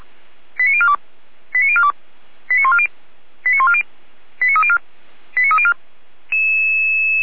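Five-tone selective call (ZVEI), heard as quick runs of five tones, tones of about 70 ms. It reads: 89042 89042 89140 89140 89304 89304
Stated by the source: ZVEI five-tone call 89305 received by SDR, Feuerwehr Orlamünde.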